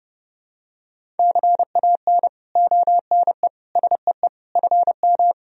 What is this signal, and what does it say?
Morse code, heard as a keyed tone, about 30 wpm, 700 Hz. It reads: CAN ONE SEE FM
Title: Sentences from Top 100 Words and Abbreviations 30wpm